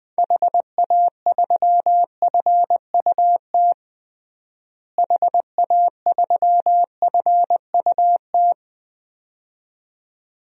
Morse code, heard as a keyed tone, about 20 wpm, 700 Hz. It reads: HA3FUT HA3FUT
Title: DXCC-100 Call Signs - 6 Characters - Repeated 2x - 20wpm